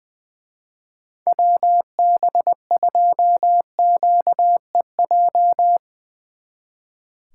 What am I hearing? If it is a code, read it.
Morse code, heard as a keyed tone, about 22 wpm, 700 Hz. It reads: WB2QEJ
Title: Speed Racing - USA 2x3 Call Signs 20wpm